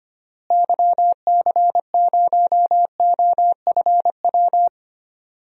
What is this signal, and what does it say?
Morse code, heard as a keyed tone, about 25 wpm, 700 Hz. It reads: YC0OFW
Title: DXCC-100 Call Signs - 6 Characters - Repeated 3x - 25wpm